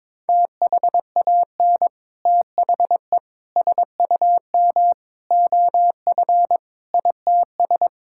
Morse code, heard as a keyed tone, about 22 wpm, 700 Hz. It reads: THAN THE SUM OF ITS
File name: practice / 22wpm / Sets of 6 Words - 22wpm